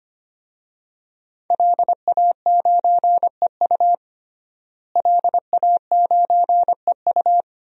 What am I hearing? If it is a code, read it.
Morse code, heard as a keyed tone, about 25 wpm, 700 Hz. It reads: LA9EU LA9EU
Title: DXCC-100 Call Signs - 5 Characters - Repeated 2x - 25wpm